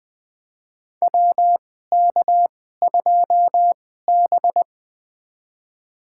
Morse code, heard as a keyed tone, about 20 wpm, spaced as for 14 wpm, 700 Hz. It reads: WK2B
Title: USA Call Signs - 4 Characters - 2x Character Spacing - 20wpm